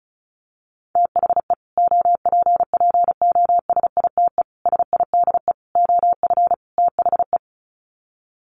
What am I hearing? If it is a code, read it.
Morse code, heard as a keyed tone, about 34 wpm, 700 Hz. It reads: THE OPPOSITE SIDE OF THE
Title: Sets of 5 Words Encore 35wpm